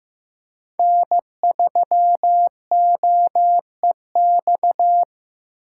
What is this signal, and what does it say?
Morse code, heard as a keyed tone, about 18 wpm, 700 Hz. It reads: N3OEX
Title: Speed Racing - USA Call Signs 15wpm